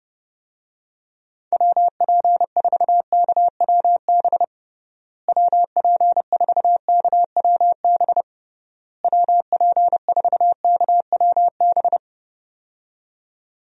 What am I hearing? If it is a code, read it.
Morse code, heard as a keyed tone, about 30 wpm, 700 Hz. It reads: WP4KWB WP4KWB WP4KWB
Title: USA Call Signs - 6 Characters - Repeated 3x - 30wpm